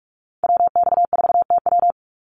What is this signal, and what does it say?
Morse code, heard as a keyed tone, about 44 wpm, 700 Hz. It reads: WX4TW
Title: NA SOTA Activator Callsigns 45wpm